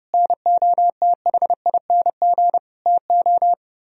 Morse code, heard as a keyed tone, about 37 wpm, 700 Hz. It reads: NOTHING TO
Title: Speed-Racing - Sets of 5 Words 30wpm